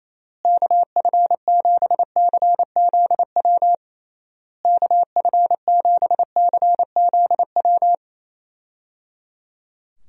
Morse code, heard as a keyed tone, about 28 wpm, 700 Hz. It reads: KF7CZW KF7CZW
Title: USA Call Signs - 6 Characters - Repeated 3x - 28wpm